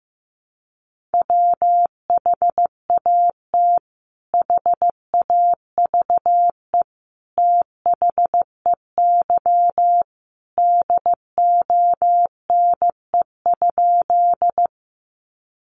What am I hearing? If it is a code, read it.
Morse code, heard as a keyed tone, about 15 wpm, 700 Hz. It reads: WHAT HAVE THEY DONE?
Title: Sentences from Top 500 Words 15wpm